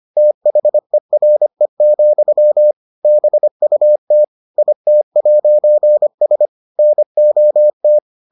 Morse code, heard as a keyed tone, about 25 wpm, 600 Hz. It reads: THERE, BUT IT'S NOT